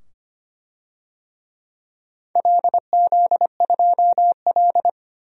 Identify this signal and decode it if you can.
Morse code, heard as a keyed tone, about 25 wpm, 700 Hz. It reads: LZ2L
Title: DXCC-100 Call Signs - 4 Characters - 25wpm